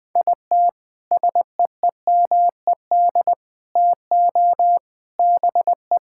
Morse code, heard as a keyed tone, about 21 wpm, 700 Hz. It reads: IT SEEMED TO BE
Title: Sentences with 3-Word Phrases - 20wpm